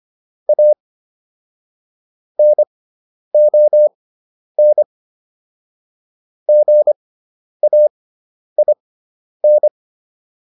Morse code, heard as a keyed tone, about 25 wpm, spaced as for 10 wpm, 600 Hz. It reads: A NON GAIN